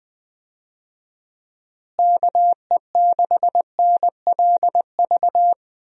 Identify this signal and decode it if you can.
Morse code, heard as a keyed tone, about 20 wpm, 700 Hz. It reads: KE6NLV
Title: USA Call Signs - 6 Characters - Repeated 3x - 20wpm